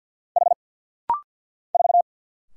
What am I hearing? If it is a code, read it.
Morse code, heard as a keyed tone, about 50 wpm, 700 Hz. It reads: H 4
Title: Single Letter-Number Alt - Rapid-Fire 50wpm